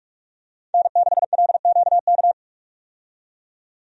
Morse code, heard as a keyed tone, about 44 wpm, 700 Hz. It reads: N6LQK
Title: USA Call Signs - 5 Characters - 45wpm